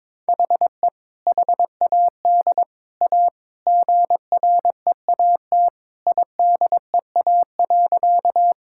Morse code, heard as a keyed tone, about 22 wpm, 700 Hz. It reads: HE HAD A GREAT IDEA.
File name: practice / 22wpm / Sentences with 3-Word Phrases - 22wpm